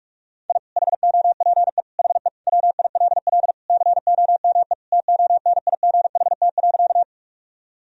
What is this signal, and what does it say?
Morse code, heard as a keyed tone, about 41 wpm, 700 Hz. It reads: I HOPE HE WILL COME TONIGHT.
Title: Sentences from Top 1000 Words - Review Part 3 - 45wpm